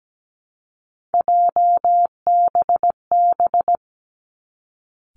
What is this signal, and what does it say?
Morse code, heard as a keyed tone, about 17 wpm, 700 Hz. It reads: JBB
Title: Three Challenging Letter-Number Combinations 17wpm